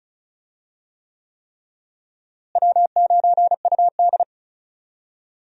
Morse code, heard as a keyed tone, about 34 wpm, 700 Hz. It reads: W9UD
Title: USA Call Signs - 4 Characters - Repeated 3x - 35wpm